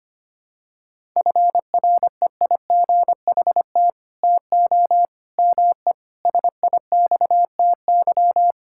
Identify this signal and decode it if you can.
Morse code, heard as a keyed tone, about 25 wpm, 700 Hz. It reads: FREIGHT TO ME SIXTY
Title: Pangram Challenge v2 - 25wpm